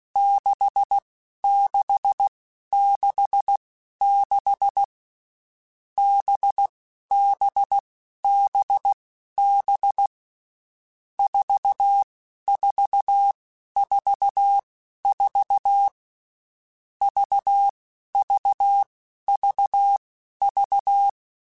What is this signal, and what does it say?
Morse code, heard as a keyed tone, about 16 wpm, spaced as for 11 wpm, 800 Hz. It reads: 6666 BBBB 4444 VVVV
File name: Groups@10 / Groups@10 mp3s / VB46@10